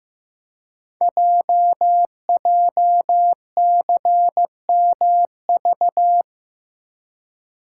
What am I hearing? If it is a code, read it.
Morse code, heard as a keyed tone, about 15 wpm, 700 Hz. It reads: JJCMV
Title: Code Groups - 5 Letters - 15wpm